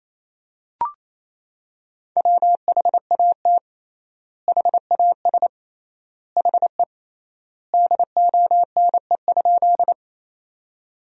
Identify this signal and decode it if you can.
Morse code, heard as a keyed tone, about 28 wpm, 700 Hz. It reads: WHAT HAS HE DONE?
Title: 3x Word-Spacing - Sentences from Top 500 Words - 28wpm